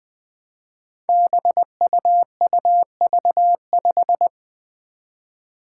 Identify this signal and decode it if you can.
Morse code, heard as a keyed tone, about 20 wpm, 700 Hz. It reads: BUUV5
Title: Five Challenging Letter-Number Combinations 20wpm